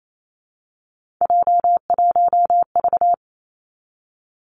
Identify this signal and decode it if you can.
Morse code, heard as a keyed tone, about 28 wpm, 690 Hz. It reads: J1V
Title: Three Challenging Letter-Number Combinations 28wpm